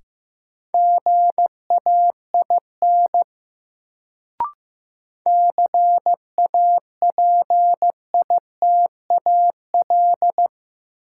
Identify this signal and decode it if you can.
Morse code, heard as a keyed tone, about 15 wpm, 700 Hz. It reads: GAIN CAPITAL
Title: Sentences from Top 1000 Words - New Word Drill - 15wpm